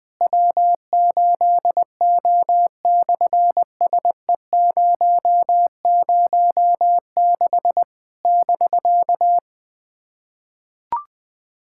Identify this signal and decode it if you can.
Morse code, heard as a keyed tone, about 20 wpm, 700 Hz. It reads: W8O/SE006 <BK>
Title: SOTA QSO Practice 20wpm